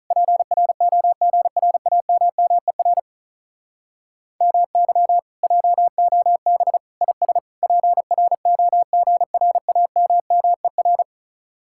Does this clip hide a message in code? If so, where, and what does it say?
Yes: Morse code from 0.1 to 11.0 s, heard as a keyed tone, about 46 wpm, 700 Hz. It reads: PROGRAMMER MY JOB IS PROGRAMMER